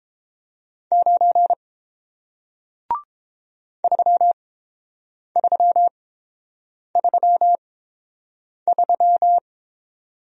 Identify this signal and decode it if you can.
Morse code, heard as a keyed tone, about 31 wpm, spaced as for 7 wpm, 690 Hz. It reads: 9 3333